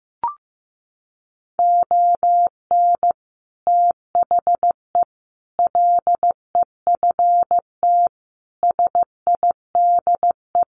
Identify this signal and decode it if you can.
Morse code, heard as a keyed tone, about 15 wpm, 700 Hz. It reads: ON THE LEFT SIDE
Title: Sets of 5 Words Encore2 15wpm